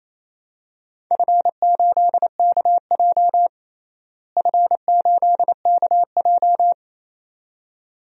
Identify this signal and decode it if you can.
Morse code, heard as a keyed tone, about 28 wpm, 700 Hz. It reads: F8KJ F8KJ
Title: DXCC-100 Call Signs - 4 Characters - Repeated 2x - 28wpm